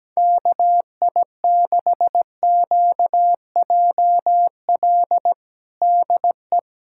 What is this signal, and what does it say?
Morse code, heard as a keyed tone, about 17 wpm, 700 Hz. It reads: KI6QJL DE
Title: USA Callsign DE Callsign - 6 Characters - 17wpm